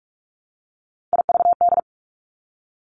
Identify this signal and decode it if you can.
Morse code, heard as a keyed tone, about 46 wpm, 700 Hz. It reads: IVD
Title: Three Challenging Letter-Number Combinations 45wpm